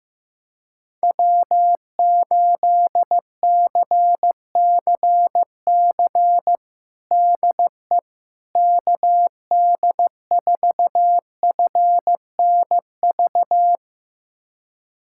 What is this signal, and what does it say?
Morse code, heard as a keyed tone, about 15 wpm, 700 Hz. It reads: W8CCC DE KD4FNV
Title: USA Callsign DE Callsign 15wpm